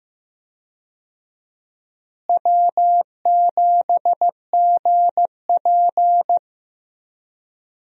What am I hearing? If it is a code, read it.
Morse code, heard as a keyed tone, about 15 wpm, 700 Hz. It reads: W7GP
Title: USA Call Signs - 4 Characters - Repeated 3x - 15wpm